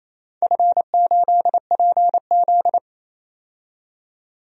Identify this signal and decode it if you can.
Morse code, heard as a keyed tone, about 28 wpm, 700 Hz. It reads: F8PZ
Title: DXCC-100 Call Signs - 4 Characters - 28wpm